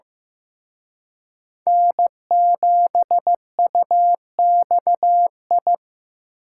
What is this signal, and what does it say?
Morse code, heard as a keyed tone, about 18 wpm, 700 Hz. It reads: N7UXI
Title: Speed Racing - USA Call Signs 15wpm